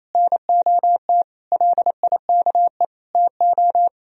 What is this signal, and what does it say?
Morse code, heard as a keyed tone, about 28 wpm, 700 Hz. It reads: NOT LIKE TO
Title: Sentences from Top 1000 Words - Review Part 4 - 28wpm